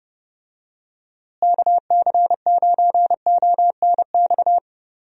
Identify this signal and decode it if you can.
Morse code, heard as a keyed tone, about 31 wpm, 700 Hz. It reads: KC9ONX
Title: USA Call Signs - 6 Characters - Repeated 3x - 30wpm